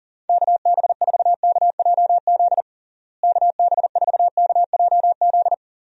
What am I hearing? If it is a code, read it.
Morse code, heard as a keyed tone, about 37 wpm, 700 Hz. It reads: KB4KJZ KB4KJZ
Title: USA Call Signs - 6 Characters - Repeated 3x - 40wpm